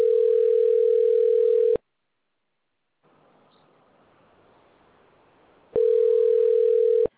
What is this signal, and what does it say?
Ringback tone (North America), 440 + 480 Hz.